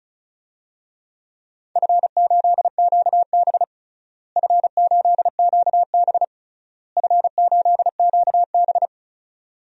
Morse code, heard as a keyed tone, about 34 wpm, 700 Hz. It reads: F8QB F8QB F8QB
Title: DXCC-100 Call Signs - 4 Characters - Repeated 3x - 35wpm